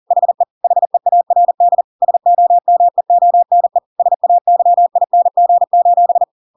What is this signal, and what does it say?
Morse code, heard as a keyed tone, about 40 wpm, 700 Hz. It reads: HE HEARD SOMEONE SAYING:
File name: Pinocchio / Pinocchio ch4 40 40 ik1hge